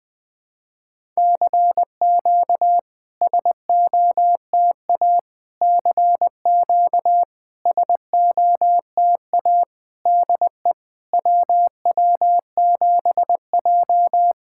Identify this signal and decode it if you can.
Morse code, heard as a keyed tone, about 20 wpm, 660 Hz. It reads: CQ SOTA CQ SOTA DE WW7J
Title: SOTA QSO Practice 20wpm